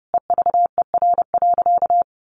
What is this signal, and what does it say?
Morse code, heard as a keyed tone, about 30 wpm, 700 Hz. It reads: EVER.
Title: Sentences from Top 1400 Words 30wpm